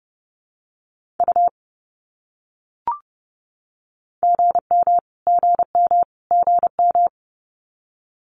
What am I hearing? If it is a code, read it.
Morse code, heard as a keyed tone, about 30 wpm, 700 Hz. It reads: U GM GM GM
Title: Abbreviations Repeated 3x - 30wpm